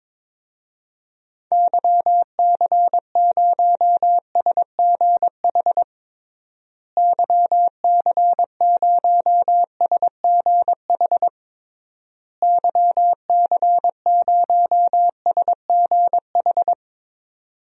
Morse code, heard as a keyed tone, about 22 wpm, 700 Hz. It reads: YC0SGH YC0SGH YC0SGH
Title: DXCC-100 Call Signs - 6 Characters - Repeated 3x - 22wpm